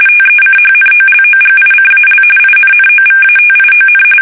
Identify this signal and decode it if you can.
RTTY, about 45.45 baud. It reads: TECHNOLOGICAL INNOVATION
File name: fsk850 / usnavy3